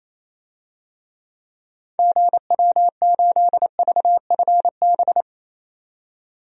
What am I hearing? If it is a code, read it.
Morse code, heard as a keyed tone, about 28 wpm, 700 Hz. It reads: GW8VFB
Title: DXCC-100 Call Signs - 6 Characters - 28wpm